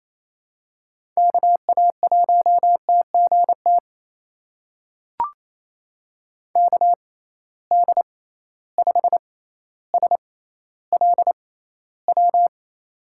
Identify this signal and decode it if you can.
Morse code, heard as a keyed tone, about 28 wpm, spaced as for 14 wpm, 700 Hz. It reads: KA1TGT KD5SLW
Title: USA Call Signs - 6 Characters - 6x Character Spacing - 28wpm